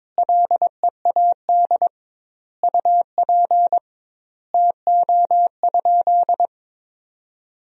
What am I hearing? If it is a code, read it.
Morse code, heard as a keyed tone, about 22 wpm, 700 Hz. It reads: LEAD UP TO?
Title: 2x Word-Spacing - Sentences from Top 500 Words - 22wpm